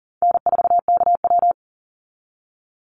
Morse code, heard as a keyed tone, about 40 wpm, 700 Hz. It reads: N4KW
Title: Easy Call Signs - 40wpm